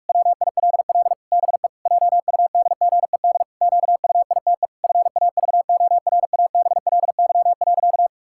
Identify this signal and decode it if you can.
Morse code, heard as a keyed tone, about 43 wpm, 700 Hz. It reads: WILL BE JUDGED QUITE FAVORABLY.
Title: Pangram Challenge v2 - 45wpm